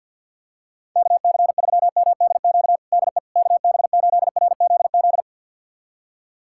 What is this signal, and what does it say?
Morse code, heard as a keyed tone, about 50 wpm, 700 Hz. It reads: KC3MDQ DE KB8RZZ
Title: USA Callsign DE Callsign 50wpm